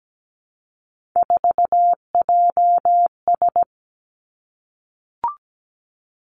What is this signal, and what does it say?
Morse code, heard as a keyed tone, about 17 wpm, 700 Hz. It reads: 4JS
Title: Three Challenging Letter-Number Combinations 17wpm